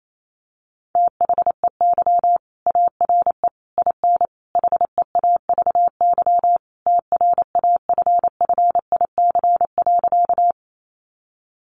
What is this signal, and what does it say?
Morse code, heard as a keyed tone, about 28 wpm, 700 Hz. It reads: THEY ARE IN HEAVY TRAFFIC.
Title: Sentences from Top 1300 Words 28wpm